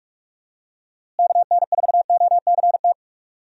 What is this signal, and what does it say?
Morse code, heard as a keyed tone, about 46 wpm, 700 Hz. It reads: KN4OCT
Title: USA Call Signs - 6 Characters - Repeated 3x - 45wpm